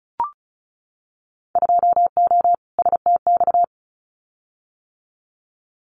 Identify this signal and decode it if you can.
Morse code, heard as a keyed tone, about 34 wpm, 700 Hz. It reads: 2O STX
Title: Winter Field Day Exchanges 35wpm